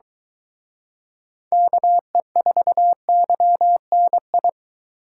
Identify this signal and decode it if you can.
Morse code, heard as a keyed tone, about 23 wpm, 700 Hz. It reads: KE4YNI